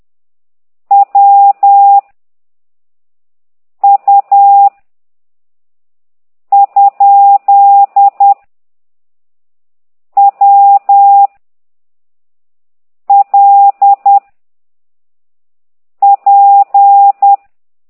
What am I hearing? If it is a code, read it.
Morse code, heard as a keyed tone, about 10 wpm, 800 Hz. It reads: W U ? W L P
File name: ct2idw / exerc8